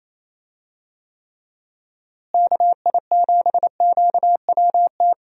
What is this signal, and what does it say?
Morse code, heard as a keyed tone, about 28 wpm, 700 Hz. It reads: KI7QWT
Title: USA Call Signs - 6 Characters - Repeated 2x - 28wpm